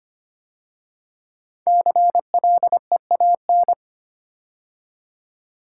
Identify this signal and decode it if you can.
Morse code, heard as a keyed tone, about 25 wpm, 700 Hz. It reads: CLEAN
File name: 5 Letter Words - First 3 Letters Given - 25wpm